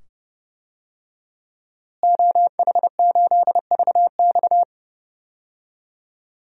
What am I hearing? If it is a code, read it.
Morse code, heard as a keyed tone, about 30 wpm, 700 Hz. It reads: OH8VX
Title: DXCC-100 Call Signs - 5 Characters - Repeated 2x - 30wpm